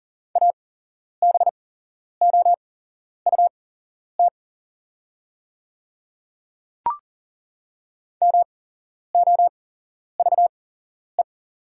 Morse code, heard as a keyed tone, about 40 wpm, spaced as for 11 wpm, 690 Hz. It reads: ABOUT MOVE
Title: 8x Character Spacing - Top 300 Words - 40wpm